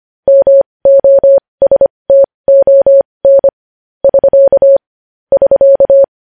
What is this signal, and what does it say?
Morse code, heard as a keyed tone, about 25 wpm, spaced as for 20 wpm, 550 Hz. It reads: MOSTON <SK> <SK>
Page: News Headlines in Morse Code at 20 WPM